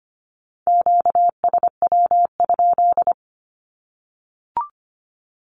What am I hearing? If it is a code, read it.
Morse code, heard as a keyed tone, about 25 wpm, 700 Hz. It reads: QSW?
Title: Q-Codes 25wpm